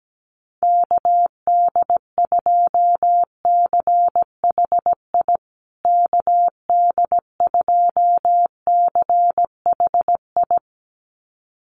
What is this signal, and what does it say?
Morse code, heard as a keyed tone, about 17 wpm, 700 Hz. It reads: KD2CHI KD2CHI
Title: USA Call Signs Repeated 2x 17wpm